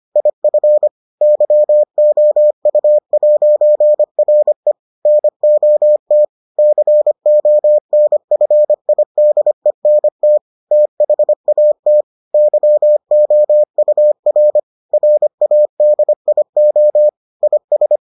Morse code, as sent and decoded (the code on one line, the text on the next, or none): .. ..-. / -.-- --- ..- .----. .-. / -. --- - / -.-. --- -. ..-. .. -.. -. - / - .... .- - / -.-- --- ..- .-. / .-. .- -.. .. --- / .. ...
IF YOU'RE NOT CONFIDENT THAT YOUR RADIO IS